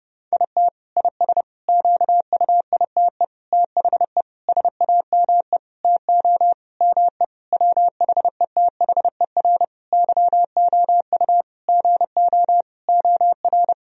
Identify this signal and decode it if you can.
Morse code, heard as a keyed tone, about 32 wpm, 700 Hz. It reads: IT IS QUITE THE SAME TO ME WHETHER YOU GO OR
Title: Sentences from Top 1000 Words - Review Part 3 - 30wpm